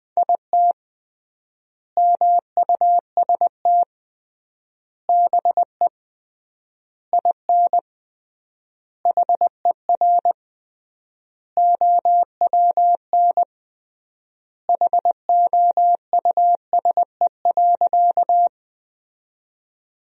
Morse code, heard as a keyed tone, about 19 wpm, 700 Hz. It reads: IT MUST BE IN HER OWN HOUSE.
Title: 3x Word-Spacing - Sentences from Top 300 Words - 20wpm